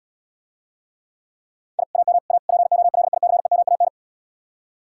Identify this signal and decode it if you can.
Morse code, heard as a keyed tone, about 77 wpm, 700 Hz. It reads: I AM A POLICEMAN